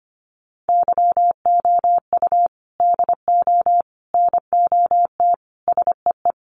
Morse code, heard as a keyed tone, about 25 wpm, 700 Hz. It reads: YOU DO NOT SEE
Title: Sentences from Top 1300 Words 25wpm